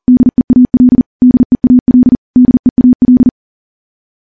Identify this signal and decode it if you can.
Morse code, heard as a keyed tone, about 40 wpm, 250 Hz. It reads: DEAL DEAL DEAL
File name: morse / morse-training-40WPM